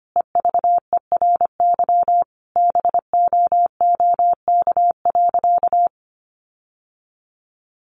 Morse code, heard as a keyed tone, about 25 wpm, 700 Hz. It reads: EVERY BOOK.
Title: Sentences from Top 1500 Words 25wpm